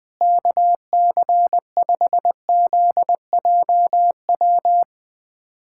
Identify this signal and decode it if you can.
Morse code, heard as a keyed tone, about 20 wpm, 700 Hz. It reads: KC5ZJW
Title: USA Callsign DE Callsign 20wpm